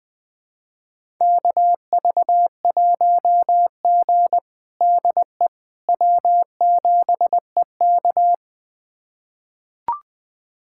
Morse code, heard as a keyed tone, about 20 wpm, 700 Hz. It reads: KV1G DE W7EK
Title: USA Callsign DE Callsign - 4 Characters - 20wpm